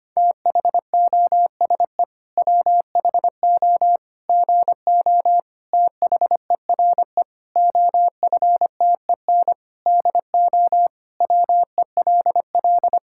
Morse code, heard as a keyed tone, about 25 wpm, 700 Hz. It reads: THOSE WHO GO THERE OFTEN DO WELL
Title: Sentences from Top 1000 Words - Review Part 5 - 25wpm